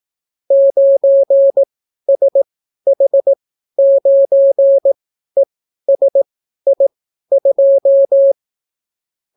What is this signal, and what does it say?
Morse code, heard as a keyed tone, about 18 wpm, spaced as for 12 wpm, 550 Hz. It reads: 9SH9ESI2